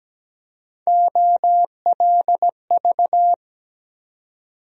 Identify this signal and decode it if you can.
Morse code, heard as a keyed tone, about 17 wpm, 700 Hz. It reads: OLV
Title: USA Call Sign Postfixes - 3 Letters 17wpm